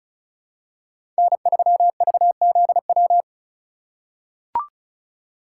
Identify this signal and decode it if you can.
Morse code, heard as a keyed tone, about 34 wpm, 700 Hz. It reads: N3VZW
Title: USA Call Signs - 5 Characters - 2x Character Spacing - 35wpm